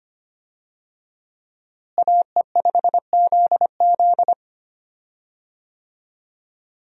Morse code, heard as a keyed tone, about 25 wpm, 700 Hz. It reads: AE5ZZ
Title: USA Call Signs - 5 Characters - Repeated 2x - 25wpm